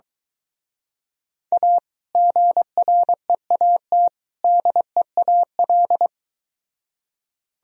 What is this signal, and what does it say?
Morse code, heard as a keyed tone, about 23 wpm, 700 Hz. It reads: A GREAT DEAL